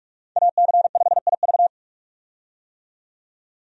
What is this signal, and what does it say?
Morse code, heard as a keyed tone, about 46 wpm, 700 Hz. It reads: AC5IV
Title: USA Call Signs - 5 Characters - 45wpm